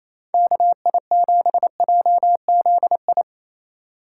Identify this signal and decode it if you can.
Morse code, heard as a keyed tone, about 36 wpm, 700 Hz. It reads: KI7JZI